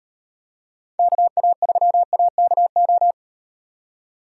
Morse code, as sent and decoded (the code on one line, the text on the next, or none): -.- .- ...-- .- -.- ---
KA3AKO